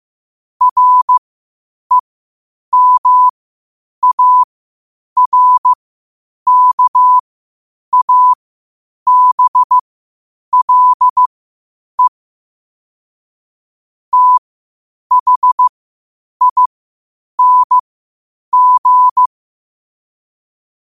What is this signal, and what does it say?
Morse code, heard as a keyed tone, about 15 wpm, spaced as for 8 wpm, 1000 Hz. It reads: REMARKABLE THING
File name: qotdfiles / 8wpm